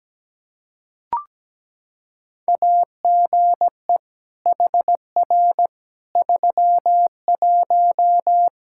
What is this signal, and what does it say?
Morse code, heard as a keyed tone, about 17 wpm, 700 Hz. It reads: AGE HR 31
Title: Practice QSO - Exchange 3 - More Info - 17wpm